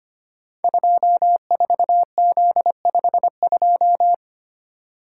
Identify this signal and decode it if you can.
Morse code, heard as a keyed tone, about 25 wpm, 700 Hz. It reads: 24Z52
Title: Code Groups - 5 Letter:Numbers - 25wpm